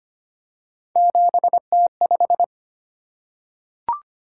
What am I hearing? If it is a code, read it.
Morse code, heard as a keyed tone, about 25 wpm, 700 Hz. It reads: 7T5
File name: Three Letter-Number Combinations - 25wpm